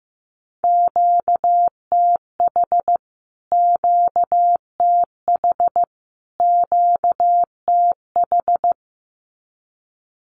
Morse code, heard as a keyed tone, about 15 wpm, 700 Hz. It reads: QTH QTH QTH
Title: Q-Codes Repeated 3x 15wpm